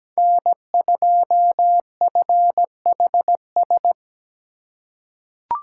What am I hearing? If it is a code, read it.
Morse code, heard as a keyed tone, about 17 wpm, 700 Hz. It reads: N2FHS
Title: USA Call Signs - 5 Characters - 6x Character Spacing - 17wpm